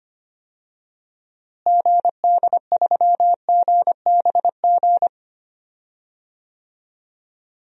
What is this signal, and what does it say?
Morse code, heard as a keyed tone, about 25 wpm, 700 Hz. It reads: GD3GBG
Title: DXCC-100 Call Signs - 6 Characters - Repeated 3x - 25wpm